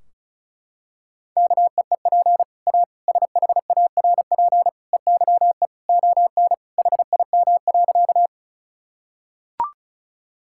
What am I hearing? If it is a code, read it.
Morse code, heard as a keyed tone, about 36 wpm, 700 Hz. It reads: KEEP A SHARP EYE ON HIM.
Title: Sentences from Top 1600 Words 35wpm